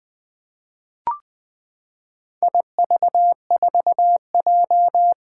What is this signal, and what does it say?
Morse code, heard as a keyed tone, about 20 wpm, 700 Hz. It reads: IV4J
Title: Four Challenging Letter-Number Combinations 20wpm